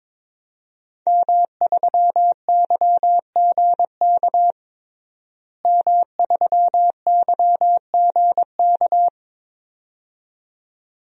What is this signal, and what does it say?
Morse code, heard as a keyed tone, about 21 wpm, 700 Hz. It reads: M3YGK M3YGK
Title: DXCC-100 Call Signs - 5 Characters - Repeated 2x - 22wpm